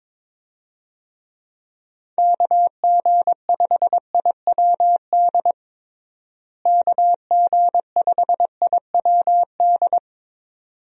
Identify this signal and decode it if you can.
Morse code, heard as a keyed tone, about 22 wpm, 700 Hz. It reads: KG5IWD KG5IWD
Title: USA Call Signs - 6 Characters - Repeated 2x - 22wpm